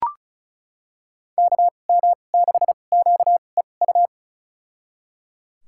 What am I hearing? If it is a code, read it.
Morse code, heard as a keyed tone, about 36 wpm, spaced as for 25 wpm, 700 Hz. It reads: KM6QEU
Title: USA Call Signs - 2x Character Spacing - 35wpm